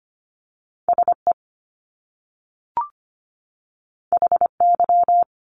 Morse code, heard as a keyed tone, about 25 wpm, 700 Hz. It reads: SE HY
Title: Top Two Letter Combinations in Words 25wpm